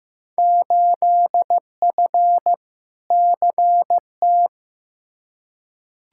Morse code, heard as a keyed tone, about 15 wpm, 700 Hz. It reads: 8F CT
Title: Field Day Exchanges Encore - 15wpm